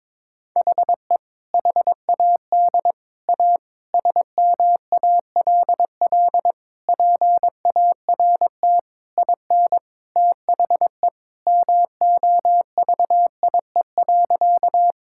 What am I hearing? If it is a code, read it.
Morse code, heard as a keyed tone, about 23 wpm, 700 Hz. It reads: HE HAD A SMALL PART IN THE MOVIE.
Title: Sentences from Top 1000 Words - Review Part 5 - 22wpm